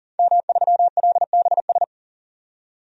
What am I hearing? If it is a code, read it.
Morse code, heard as a keyed tone, about 40 wpm, 700 Hz. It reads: M3LBS
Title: DXCC-100 Call Signs - 5 Characters - 40wpm